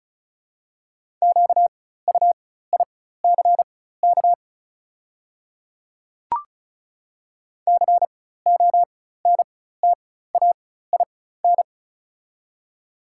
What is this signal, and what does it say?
Morse code, heard as a keyed tone, about 34 wpm, spaced as for 16 wpm, 690 Hz. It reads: QUICK CONTAIN
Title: 4x Character Spacing - Top 500 Words - 35wpm